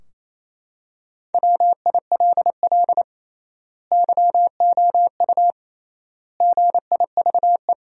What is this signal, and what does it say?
Morse code, heard as a keyed tone, about 28 wpm, 700 Hz. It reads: WILL YOU GIVE
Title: 3x Word-Spacing - Sentences from Top 200 Words - 28wpm